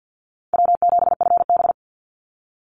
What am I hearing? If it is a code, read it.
Morse code, heard as a keyed tone, about 50 wpm, 700 Hz. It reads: W7FB
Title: Call Signs - 50wpm